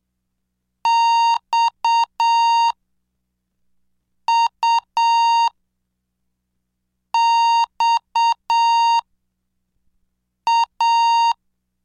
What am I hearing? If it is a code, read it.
Morse code, heard as a keyed tone, about 7 wpm, spaced as for 4 wpm, 930 Hz. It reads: XUXA
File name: cursos / CDestudo / 17 Track 17